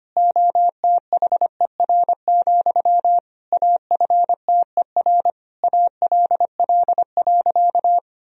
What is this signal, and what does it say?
Morse code, heard as a keyed tone, about 25 wpm, 700 Hz. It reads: OTHER, AFTER ALL.
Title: Sentences from Top 1000 Words - Review Part 1 - 25wpm